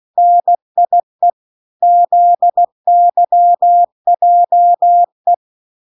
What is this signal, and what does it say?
Morse code, heard as a keyed tone, about 16 wpm, 700 Hz. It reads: NIE ZYJE